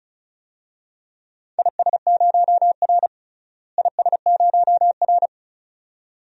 Morse code, heard as a keyed tone, about 34 wpm, 700 Hz. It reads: IS0R IS0R
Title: DXCC-100 Call Signs - 4 Characters - Repeated 2x - 35wpm